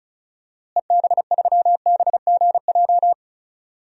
Morse code, heard as a keyed tone, about 34 wpm, 700 Hz. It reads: EB3BGJ
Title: DXCC-100 Call Signs - 6 Characters - Repeated 3x - 35wpm